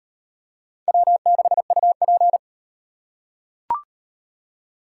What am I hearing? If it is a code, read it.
Morse code, heard as a keyed tone, about 37 wpm, 700 Hz. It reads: W6UP